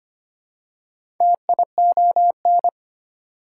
Morse code, heard as a keyed tone, about 25 wpm, 700 Hz. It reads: TION
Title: Prefixes and Suffixes 25wpm